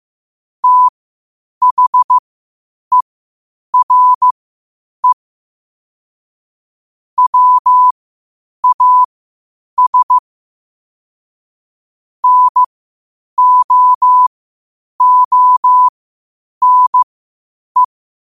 Morse code, heard as a keyed tone, about 15 wpm, spaced as for 8 wpm, 1000 Hz. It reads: THERE WAS NO ONE
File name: qotdfiles / 8wpm